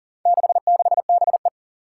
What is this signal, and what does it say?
Morse code, heard as a keyed tone, about 40 wpm, 700 Hz. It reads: 66BE
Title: Four Challenging Letter-Number Combinations 40wpm